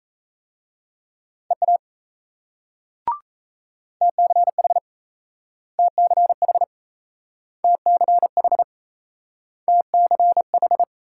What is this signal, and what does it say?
Morse code, heard as a keyed tone, about 37 wpm, 700 Hz. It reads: EA TCH TCH TCH TCH